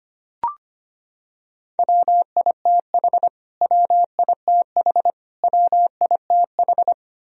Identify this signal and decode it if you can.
Morse code, heard as a keyed tone, about 25 wpm, 700 Hz. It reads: WITH WITH WITH
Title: Top 100 Words Repeated 25wpm